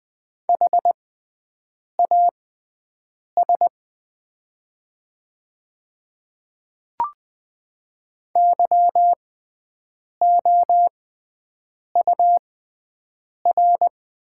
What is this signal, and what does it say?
Morse code, heard as a keyed tone, about 20 wpm, spaced as for 7 wpm, 700 Hz. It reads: HAS YOUR
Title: 6x Character Spacing - Top 100 Words - 20wpm